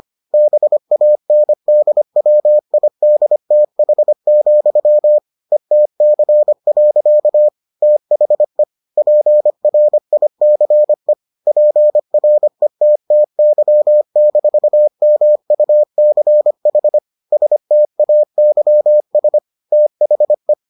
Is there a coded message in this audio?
Yes: Morse code from 0.3 to 20.5 s, heard as a keyed tone, about 25 wpm, 600 Hz. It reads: BANDWIDTH, ETC. THE PRICE PRETTY-MUCH STAYS THE